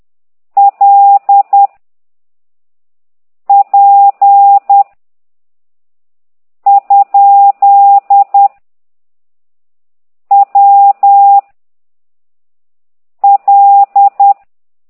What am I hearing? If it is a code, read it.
Morse code, heard as a keyed tone, about 10 wpm, 800 Hz. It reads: L P ? W L
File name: ct2idw / exerc8